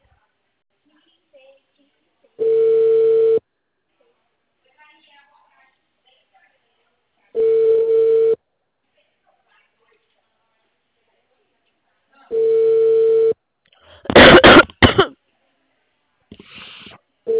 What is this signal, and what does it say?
Ringback tone (China, 450 Hz), with speech.
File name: out- -2007- - - .15695